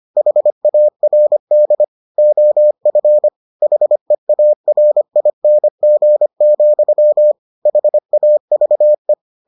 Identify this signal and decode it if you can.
Morse code, heard as a keyed tone, about 25 wpm, 600 Hz. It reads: HARD OF HEARING, HAVE